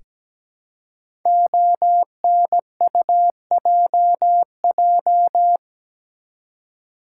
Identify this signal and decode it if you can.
Morse code, heard as a keyed tone, about 17 wpm, 700 Hz. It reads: ONUJJ
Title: Code Groups - 5 Letters - 17wpm